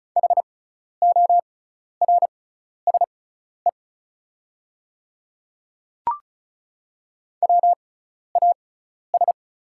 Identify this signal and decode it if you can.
Morse code, heard as a keyed tone, about 35 wpm, spaced as for 12 wpm, 690 Hz. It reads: HORSE WAS
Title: 6x Character Spacing - Top 300 Words - 35wpm